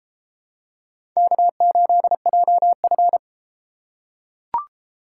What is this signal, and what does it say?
Morse code, heard as a keyed tone, about 30 wpm, 700 Hz. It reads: K8JF